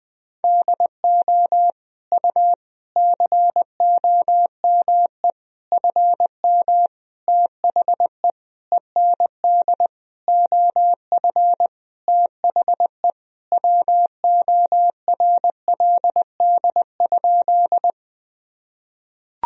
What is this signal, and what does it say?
Morse code, heard as a keyed tone, about 20 wpm, 700 Hz. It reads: DO U COME FM THE END OF THE WORLD?
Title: Sentences from Top 200 Words and Abbreviations 20wpm